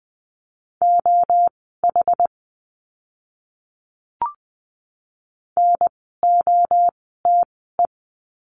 Morse code, heard as a keyed tone, about 20 wpm, spaced as for 14 wpm, 700 Hz. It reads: OH NOTE
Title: 061 Lesson 11f - Letter h - Words 20wpm